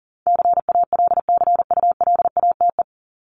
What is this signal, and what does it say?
Morse code, heard as a keyed tone, about 40 wpm, 700 Hz. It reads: CALCULATE
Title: Top 1751-2000 Words 40wpm